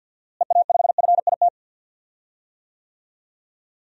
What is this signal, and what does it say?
Morse code, heard as a keyed tone, about 47 wpm, 700 Hz. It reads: EA5FIT
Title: DXCC-100 Call Signs - 6 Characters - Repeated 3x - 50wpm